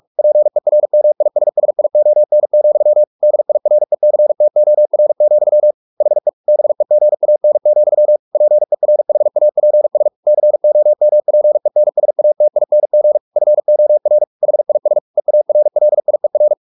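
Morse code, heard as a keyed tone, about 44 wpm, 600 Hz. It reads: PERMISSION, DIRECTOR, HE BEGAN, PERHAPS COMPENSATING FOR HIS EARLIER
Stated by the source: Morse practice at 45 wpm, the novel Never by Ken Follett in English, from CW med Gnister.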